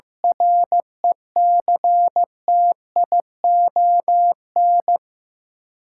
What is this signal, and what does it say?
Morse code, heard as a keyed tone, about 16 wpm, 700 Hz. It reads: RECTION
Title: Sentences from Top 1300 Words - New Word Drill Alt - 15wpm